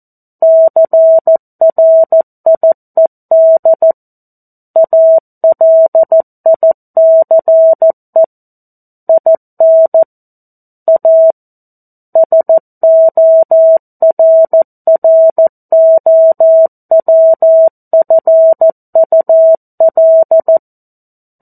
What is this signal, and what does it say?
Morse code, heard as a keyed tone, about 14 wpm, 650 Hz. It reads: CRIED ALICE IN A SORROWFUL
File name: 02-Chapter-02 14 WPM